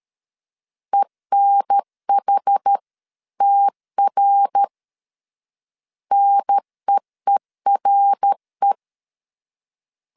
Morse code, heard as a keyed tone, about 13 wpm, 800 Hz. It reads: ENHTR NEERE
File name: CW / DET / 1 3 1500C S B 12wpm 2828